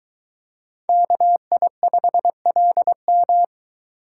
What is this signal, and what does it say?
Morse code, heard as a keyed tone, about 22 wpm, 700 Hz. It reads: KI5LM